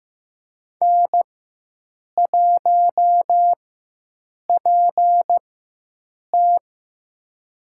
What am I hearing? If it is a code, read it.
Morse code, heard as a keyed tone, about 15 wpm, spaced as for 7 wpm, 700 Hz. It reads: N1PT
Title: USA Call Signs - 4 Characters - 4x Character Spacing - 15wpm